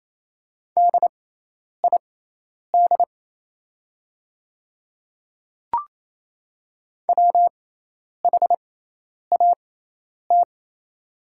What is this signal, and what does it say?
Morse code, heard as a keyed tone, about 28 wpm, spaced as for 10 wpm, 700 Hz. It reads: DID WHAT
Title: 6x Character Spacing - Top 100 Words - 28wpm